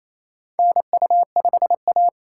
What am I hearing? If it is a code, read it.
Morse code, heard as a keyed tone, about 28 wpm, 700 Hz. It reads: NU5A
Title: K1USN SST QSO Practice - 28wpm